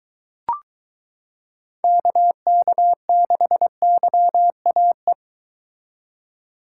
Morse code, heard as a keyed tone, about 23 wpm, 700 Hz. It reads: KK6YAE